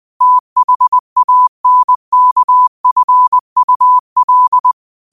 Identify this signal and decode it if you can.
Morse code, heard as a keyed tone, about 20 wpm, 1000 Hz. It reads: THANKFUL